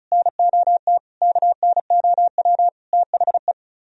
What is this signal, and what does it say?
Morse code, heard as a keyed tone, about 46 wpm, spaced as for 41 wpm, 700 Hz. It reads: NOT KNOW THE
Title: Speed-Racing - Sets of 5 Words 35wpm